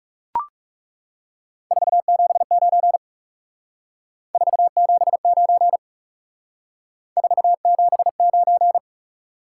Morse code, heard as a keyed tone, about 38 wpm, 700 Hz. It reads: 479 479 479
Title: Speed-Racing Top 180 QSO Elements 30wpm